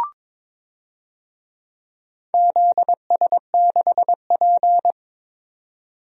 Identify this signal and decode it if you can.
Morse code, heard as a keyed tone, about 21 wpm, 700 Hz. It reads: ZS6P
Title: DXCC-100 Call Signs - 4 Characters - Repeated 3x - 22wpm